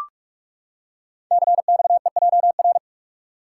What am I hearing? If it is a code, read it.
Morse code, heard as a keyed tone, about 45 wpm, 700 Hz. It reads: CXEJR
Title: Code Groups - 5 Letter:Numbers - 45wpm